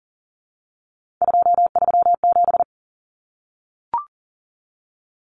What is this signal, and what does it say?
Morse code, heard as a keyed tone, about 40 wpm, 700 Hz. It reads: 237
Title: Numbers - 3-Digit 40wpm